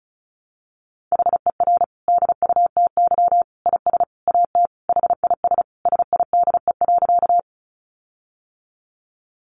Morse code, heard as a keyed tone, about 34 wpm, 700 Hz. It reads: HER DUTY IS AT HIS SIDE.
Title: Sentences from Top 1500 Words 35wpm